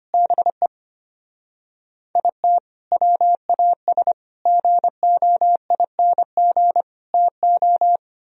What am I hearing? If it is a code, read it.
Morse code, heard as a keyed tone, about 31 wpm, 700 Hz. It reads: BE IT WAS GOING TO
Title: Speed-Racing - Sets of 5 Words 25wpm